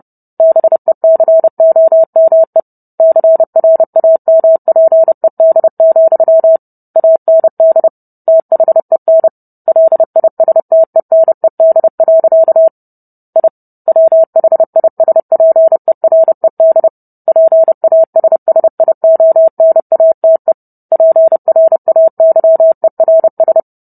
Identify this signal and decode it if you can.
Morse code, heard as a keyed tone, about 30 wpm, 650 Hz. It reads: BECOME CRAMPED, AND THEN LISTENED. I WHISPERED PASSIONATE PRAYERS